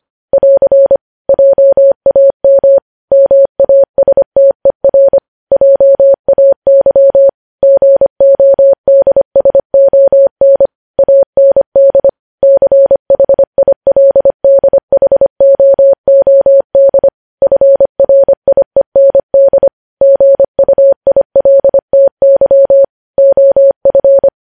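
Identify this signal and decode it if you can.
Morse code, heard as a keyed tone, about 25 wpm, 550 Hz. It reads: <AR> JAM MASTER JAY GODSON AND CHILDHOOD FRIEND GUILTY OF
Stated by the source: News headlines sent in Morse code at 25 wpm.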